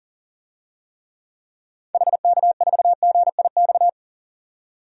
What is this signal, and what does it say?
Morse code, heard as a keyed tone, about 37 wpm, 700 Hz. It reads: HK4GIX
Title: DXCC-100 Call Signs - 6 Characters - 40wpm